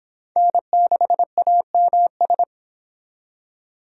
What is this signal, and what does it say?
Morse code, heard as a keyed tone, about 26 wpm, 700 Hz. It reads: N6AMS